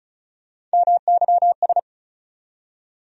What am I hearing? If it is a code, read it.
Morse code, heard as a keyed tone, about 36 wpm, 700 Hz. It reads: MYS
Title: USA Call Sign Postfixes - 3 Letters 35wpm